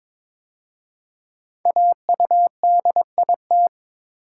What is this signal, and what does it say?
Morse code, heard as a keyed tone, about 22 wpm, 700 Hz. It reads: AUDIT
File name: 5 Letter Words - First 3 Letters Given - 22wpm